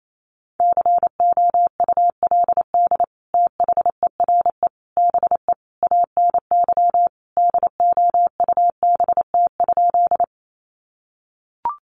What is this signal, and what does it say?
Morse code, heard as a keyed tone, about 28 wpm, 700 Hz. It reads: COULD THERE BE ANY DOUBT?
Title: Sentences from Top 1500 Words 28wpm